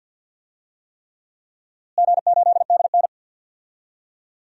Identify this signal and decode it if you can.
Morse code, heard as a keyed tone, about 52 wpm, 700 Hz. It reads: G8DN
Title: DXCC-100 Call Signs - 4 Characters - Repeated 3x - 50wpm